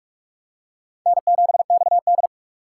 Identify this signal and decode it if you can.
Morse code, heard as a keyed tone, about 44 wpm, 690 Hz. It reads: N7XD
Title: USA Call Signs - 4 Characters - 2x Character Spacing - 45wpm